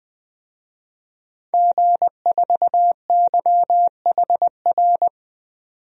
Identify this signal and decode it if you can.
Morse code, heard as a keyed tone, about 20 wpm, 700 Hz. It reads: G4YHR
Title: DXCC-100 Call Signs - 5 Characters - 20wpm